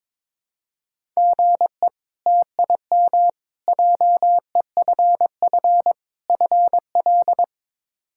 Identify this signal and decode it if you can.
Morse code, heard as a keyed tone, about 21 wpm, 700 Hz. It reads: GE TIM JEFF FL
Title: K1USN SST QSO Practice - 22wpm